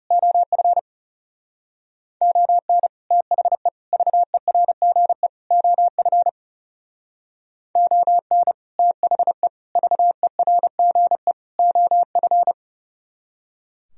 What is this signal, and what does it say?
Morse code, heard as a keyed tone, about 37 wpm, 700 Hz. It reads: OF ON THE VERGE OF ON THE VERGE OF
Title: Speed-Racing - Sets of 4 Words 30wpm